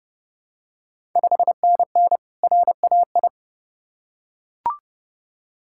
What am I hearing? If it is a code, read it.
Morse code, heard as a keyed tone, about 30 wpm, 700 Hz. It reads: 5NN RAI
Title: Ontario QSO Party Exchange 30wpm